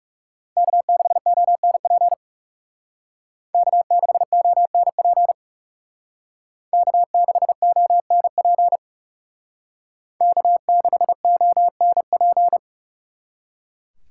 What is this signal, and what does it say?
Morse code, heard as a keyed tone, about 38 wpm, 700 Hz. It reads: K6ONP K6ONP K6ONP K6ONP